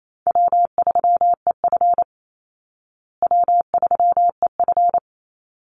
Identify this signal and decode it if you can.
Morse code, heard as a keyed tone, about 28 wpm, 700 Hz. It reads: W3EF W3EF
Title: Easy Call Signs Repeated 2x - 28wpm